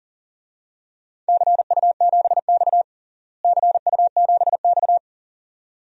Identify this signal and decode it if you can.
Morse code, heard as a keyed tone, about 37 wpm, 700 Hz. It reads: CU7X CU7X
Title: DXCC-100 Call Signs - 4 Characters - Repeated 2x - 40wpm